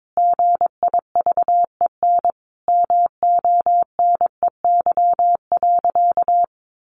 Morse code, heard as a keyed tone, about 22 wpm, 700 Hz. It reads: GIVEN MONEY.
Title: Sentences from Top 1400 Words 22wpm